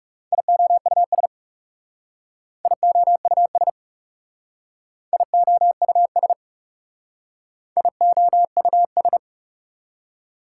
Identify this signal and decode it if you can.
Morse code, heard as a keyed tone, about 39 wpm, 690 Hz. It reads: IOUS IOUS IOUS IOUS